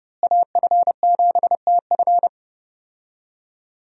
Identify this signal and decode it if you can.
Morse code, heard as a keyed tone, about 30 wpm, 700 Hz. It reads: AF7TF
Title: USA Call Signs - 5 Characters - Repeated 2x - 30wpm